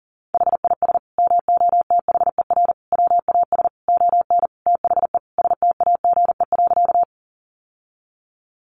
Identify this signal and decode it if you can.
Morse code, heard as a keyed tone, about 40 wpm, 700 Hz. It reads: HIS MOTHER WAS ON THE STAGE.
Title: Sentences from Top 1200 Words 40wpm